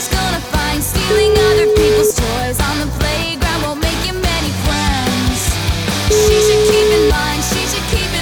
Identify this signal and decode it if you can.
Ringback tone (Europe, 425 Hz), with noise.